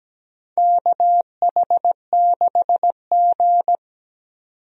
Morse code, heard as a keyed tone, about 17 wpm, 700 Hz. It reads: KH6G
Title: DXCC-100 Call Signs - 4 Characters - 17wpm